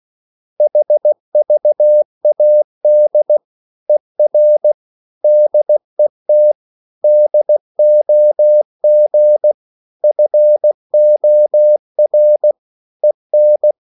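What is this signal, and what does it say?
Morse code, heard as a keyed tone, about 16 wpm, 600 Hz. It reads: HVAD ER DET DOG FOR EN
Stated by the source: Slow (QRS) Morse code at 16 wpm, from CW med Gnister.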